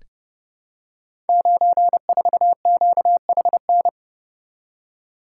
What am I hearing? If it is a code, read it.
Morse code, heard as a keyed tone, about 30 wpm, 700 Hz. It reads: 94QHN
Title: Code Groups - 5 Letter:Numbers - 30wpm